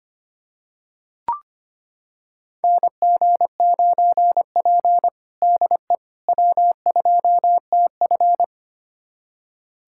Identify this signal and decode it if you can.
Morse code, heard as a keyed tone, about 25 wpm, 700 Hz. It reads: NG9P DE W2TF
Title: USA Callsign DE Callsign - 4 Characters - 25wpm